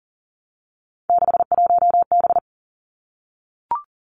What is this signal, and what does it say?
Morse code, heard as a keyed tone, about 38 wpm, 700 Hz. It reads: 61B
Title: Three Challenging Letter-Number Combinations 40wpm